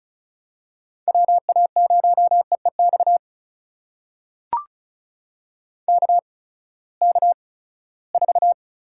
Morse code, heard as a keyed tone, about 36 wpm, spaced as for 20 wpm, 700 Hz. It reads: WA0EEX KK4
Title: USA Call Signs - 6 Characters - 8x Character Spacing - 35wpm